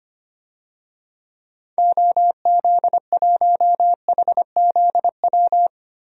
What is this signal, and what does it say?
Morse code, heard as a keyed tone, about 25 wpm, 700 Hz. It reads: OZ1HZW
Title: DXCC-100 Call Signs - 6 Characters - 25wpm